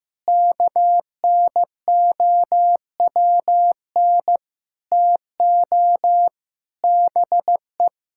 Morse code, heard as a keyed tone, about 16 wpm, 700 Hz. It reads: KNOWN TO BE
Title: Sentences from Top 1000 Words - Review Part 5 - 15wpm